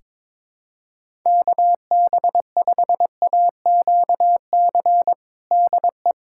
Morse code, heard as a keyed tone, about 22 wpm, 700 Hz. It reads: KB5AQC DE
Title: USA Callsign DE Callsign - 6 Characters - 22wpm